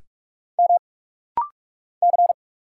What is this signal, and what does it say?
Morse code, heard as a keyed tone, about 45 wpm, 690 Hz. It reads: M C
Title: Single Letters - Rapid-Fire 45wpm